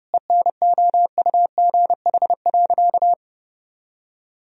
Morse code, heard as a keyed tone, about 30 wpm, 700 Hz. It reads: ENOUGH.
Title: Sentences from Top 400 Words and Abbreviations 30wpm